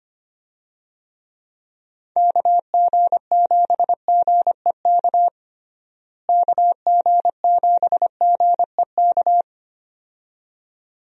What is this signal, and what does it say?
Morse code, heard as a keyed tone, about 25 wpm, 700 Hz. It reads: KG7GEK KG7GEK
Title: USA Call Signs - 6 Characters - Repeated 2x - 25wpm